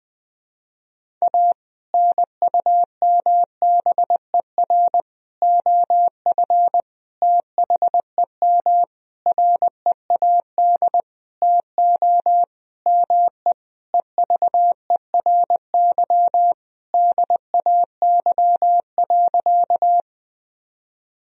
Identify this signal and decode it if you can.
Morse code, heard as a keyed tone, about 20 wpm, 700 Hz. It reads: A NUMBER OF THEM READ TO ME EVERY DAY.
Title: Sentences from Top 1000 Words - Review Part 1 - 20wpm